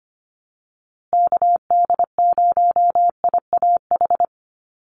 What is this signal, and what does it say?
Morse code, heard as a keyed tone, about 25 wpm, 700 Hz. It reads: KD0IAH
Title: USA Call Signs Repeated 2x 25wpm